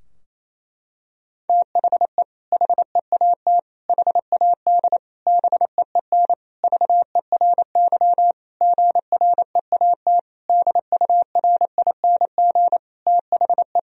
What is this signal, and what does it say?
Morse code, heard as a keyed tone, about 28 wpm, 700 Hz. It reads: THE HEAT HAD BEEN VERY GREAT DURING THE
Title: Sentences from Top 1000 Words - Review Part 3 - 28wpm